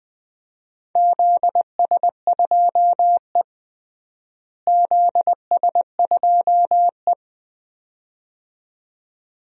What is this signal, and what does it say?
Morse code, heard as a keyed tone, about 20 wpm, 700 Hz. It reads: ZS2E ZS2E
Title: DXCC-100 Call Signs - 4 Characters - Repeated 2x - 20wpm